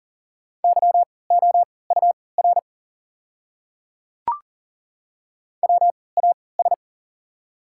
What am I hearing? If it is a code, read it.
Morse code, heard as a keyed tone, about 40 wpm, spaced as for 23 wpm, 700 Hz. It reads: YOUR WAS
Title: 3x Character Spacing - Top 300 Words - 40wpm